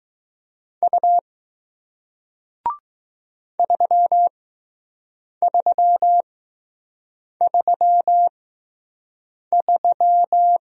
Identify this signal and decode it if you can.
Morse code, heard as a keyed tone, about 20 wpm, spaced as for 6 wpm, 700 Hz. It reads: U 3333